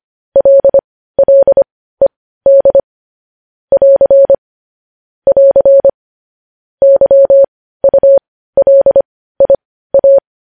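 Morse code, heard as a keyed tone, about 25 wpm, spaced as for 15 wpm, 550 Hz. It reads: LLED <AR> <AR> YULIA
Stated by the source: News headlines sent in Morse code at 15 wpm.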